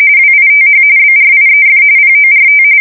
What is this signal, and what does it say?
RTTY, about 45.45 baud. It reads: WITH BRIEF PORT
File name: fsk170 / USS-New-Jersey